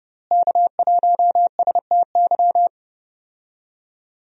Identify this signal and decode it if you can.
Morse code, heard as a keyed tone, about 30 wpm, 700 Hz. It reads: K1STY
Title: USA Call Signs - 5 Characters - Repeated 3x - 30wpm